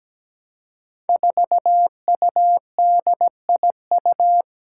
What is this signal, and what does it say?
Morse code, heard as a keyed tone, about 17 wpm, 700 Hz. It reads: 4UDIU
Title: Five Challenging Letter-Number Combinations 17wpm